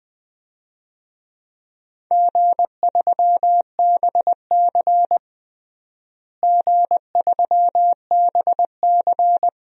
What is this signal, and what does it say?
Morse code, heard as a keyed tone, about 20 wpm, 700 Hz. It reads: G3BC G3BC
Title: DXCC-100 Call Signs - 4 Characters - Repeated 2x - 20wpm